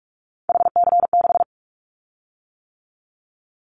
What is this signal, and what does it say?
Morse code, heard as a keyed tone, about 46 wpm, 700 Hz. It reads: HC6
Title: Callsign Prefixes for Top 100 DXCC - 45wpm